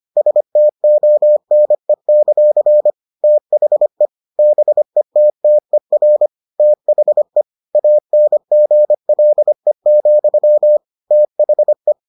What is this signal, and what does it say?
Morse code, heard as a keyed tone, about 25 wpm, 600 Hz. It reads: STONE; THE BETTER THE ANGLE, THE